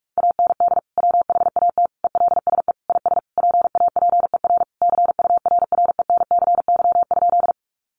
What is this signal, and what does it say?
Morse code, heard as a keyed tone, about 44 wpm, 700 Hz. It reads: AND WHAT ELSE IS PAPER CURRENCY?
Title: Sentences from Top 1500 Words 45wpm